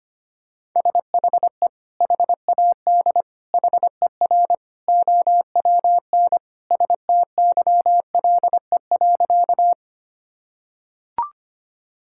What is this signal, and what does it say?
Morse code, heard as a keyed tone, about 25 wpm, 700 Hz. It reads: SHE HAD HER OWN STYLE.
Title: Sentences from Top 1000 Words - Review Part 5 - 25wpm